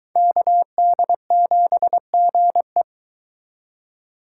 Morse code, heard as a keyed tone, about 23 wpm, 700 Hz. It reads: KD7GE